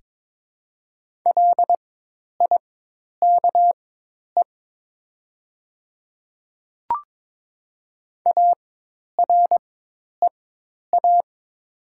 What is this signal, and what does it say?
Morse code, heard as a keyed tone, about 22 wpm, spaced as for 10 wpm, 700 Hz. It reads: LIKE AREA
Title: 4x Character Spacing - Top 500 Words - 22wpm